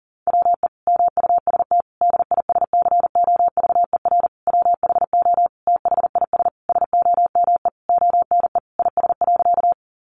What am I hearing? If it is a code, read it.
Morse code, heard as a keyed tone, about 40 wpm, 700 Hz. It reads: WE MUST DISCOVER WHO THIS SOME ONE IS.
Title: Sentences from Top 1100 Words 40wpm